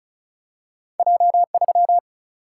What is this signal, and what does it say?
Morse code, heard as a keyed tone, about 36 wpm, 700 Hz. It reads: J3
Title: Single-Letter Single-Number 35wpm